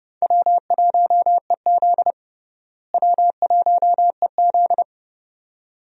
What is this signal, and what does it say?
Morse code, heard as a keyed tone, about 32 wpm, 700 Hz. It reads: W1EZ W1EZ
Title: USA Call Signs - 4 Characters - Repeated 2x - 30wpm